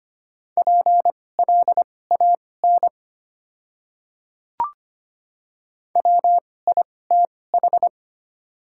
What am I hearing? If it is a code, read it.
Morse code, heard as a keyed tone, about 25 wpm, spaced as for 18 wpm, 700 Hz. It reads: PLAN WITH
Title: 2x Character Spacing - Top 500 Words - 25wpm